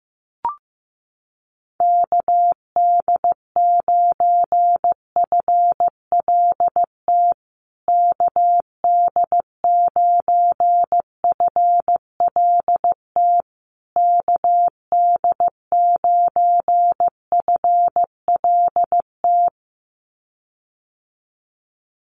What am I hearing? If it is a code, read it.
Morse code, heard as a keyed tone, about 15 wpm, 700 Hz. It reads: KD9FLT KD9FLT KD9FLT
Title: USA Call Signs Repeated 3x 15wpm